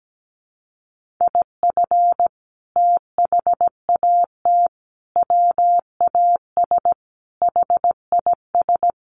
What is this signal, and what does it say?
Morse code, heard as a keyed tone, about 17 wpm, 700 Hz. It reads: IF THAT WAS HIS
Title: Sentences from Top 1400 Words 17wpm